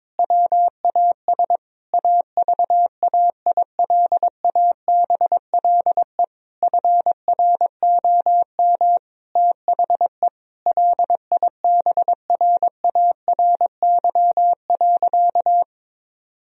Morse code, heard as a keyed tone, about 22 wpm, 700 Hz. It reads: WAS AVAILABLE FROM THE LIBRARY.
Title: Sentences from Top 1000 Words - Review Part 3 - 22wpm